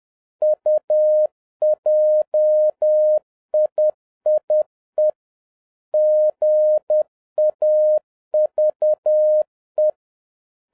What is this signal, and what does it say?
Morse code, heard as a keyed tone, about 10 wpm, 610 Hz. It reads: UJIIE GAVE